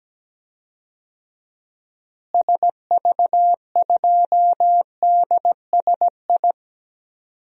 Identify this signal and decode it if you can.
Morse code, heard as a keyed tone, about 17 wpm, 700 Hz. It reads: SV2DSI
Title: DXCC-100 Call Signs - 6 Characters - Repeated 3x - 17wpm